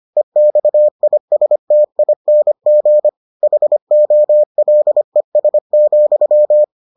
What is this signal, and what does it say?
Morse code, heard as a keyed tone, about 25 wpm, 600 Hz. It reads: EXISTING HOLES,